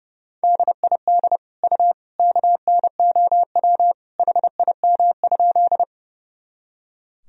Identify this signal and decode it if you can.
Morse code, heard as a keyed tone, about 30 wpm, 700 Hz. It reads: DID U KNOW HIM?
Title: Sentences from Top 100 Words and Abbreviations 30wpm